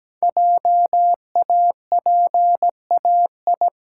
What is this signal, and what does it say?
Morse code, heard as a keyed tone, about 17 wpm, 700 Hz. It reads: JAPAI
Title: Code Groups - 5 Letter:Numbers - 17wpm